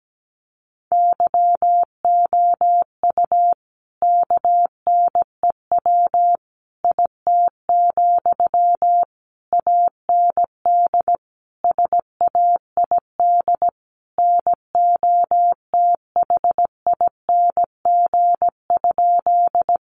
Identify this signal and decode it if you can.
Morse code, heard as a keyed tone, about 17 wpm, 700 Hz. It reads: YOU KNEW IT, AND SAID NOTHING?
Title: Sentences from Top 500 Words 17wpm